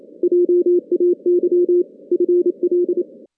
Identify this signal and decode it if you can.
Morse code, heard as a keyed tone, about 27 wpm, 410 Hz. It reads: JAY FL
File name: 2016 JAN CW NAQP / 2016 Jan naqp k5ot - 0-8